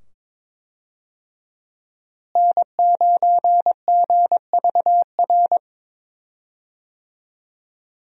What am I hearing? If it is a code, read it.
Morse code, heard as a keyed tone, about 22 wpm, 700 Hz. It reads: N9GVR
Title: USA Call Signs - 5 Characters - Repeated 2x - 22wpm